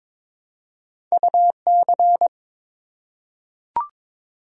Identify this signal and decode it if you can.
Morse code, heard as a keyed tone, about 22 wpm, 700 Hz. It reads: UC
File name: practice / 22wpm / Two Letter-Number Combinations - 22wpm